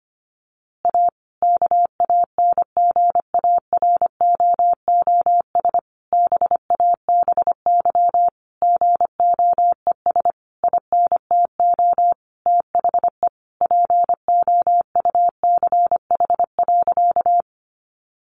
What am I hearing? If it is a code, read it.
Morse code, heard as a keyed tone, about 25 wpm, 700 Hz. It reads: A KANGAROOS BABY GOES INTO THE POUCH.
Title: Easy Phrases 25wpm